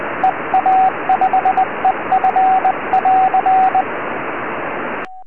Morse code, heard as a keyed tone, about 20 wpm, 760 Hz, with radio noise. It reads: EA5EF<AR>
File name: ea5ef 4m